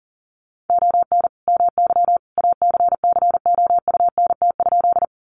Morse code, heard as a keyed tone, about 37 wpm, 700 Hz. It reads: ON MY ACCOUNT?
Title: Sentences from Top 900 Words 40wpm